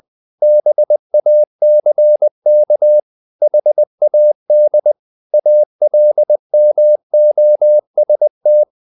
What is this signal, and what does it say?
Morse code, heard as a keyed tone, about 20 wpm, 600 Hz. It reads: BACK HAD ALMOST
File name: the old man and the sea part 2 20wpm